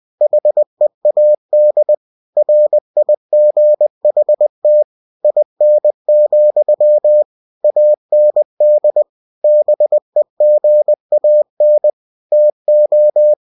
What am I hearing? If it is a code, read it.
Morse code, heard as a keyed tone, about 20 wpm, 600 Hz. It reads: HEAD RIGHT IN, AND BEGAN TO
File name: cw-books / winnie-the-pooh / winnie-20wpm-page-0040